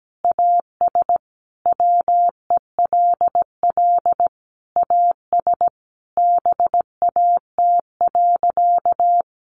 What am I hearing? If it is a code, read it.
Morse code, heard as a keyed tone, about 17 wpm, 700 Hz. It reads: AS WELL AS BAT.
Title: Sentences from Top 1500 Words 17wpm